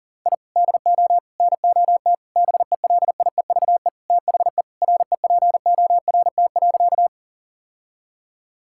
Morse code, heard as a keyed tone, about 37 wpm, 700 Hz. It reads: I DO NOT BELIEVE THE REPORT.
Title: Sentences from Top 1000 Words - Review Part 6 - 40wpm